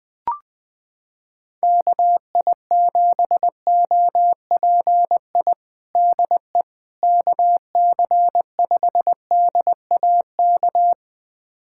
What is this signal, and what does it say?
Morse code, heard as a keyed tone, about 20 wpm, 700 Hz. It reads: KI7OPI DE KC5DAK
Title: USA Callsign DE Callsign 20wpm